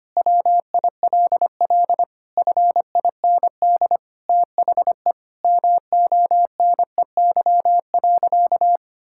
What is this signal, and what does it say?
Morse code, heard as a keyed tone, about 25 wpm, 700 Hz. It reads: WILL FIND THE MONEY.
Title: Sentences from Top 1600 Words 25wpm